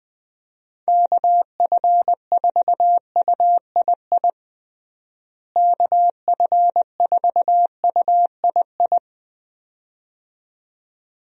Morse code, heard as a keyed tone, about 21 wpm, 700 Hz. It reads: KF4UII KF4UII
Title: USA Call Signs - 6 Characters - Repeated 2x - 20wpm